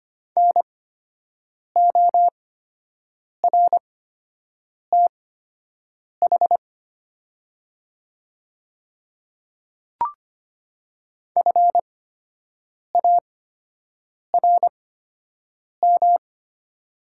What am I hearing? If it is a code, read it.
Morse code, heard as a keyed tone, about 25 wpm, spaced as for 7 wpm, 690 Hz. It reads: NORTH FARM
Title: 8x Character Spacing - Top 500 Words - 25wpm